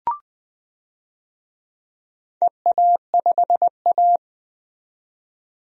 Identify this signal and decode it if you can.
Morse code, heard as a keyed tone, about 20 wpm, 700 Hz. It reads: EA5A
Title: DXCC-100 Call Signs - 4 Characters - 20wpm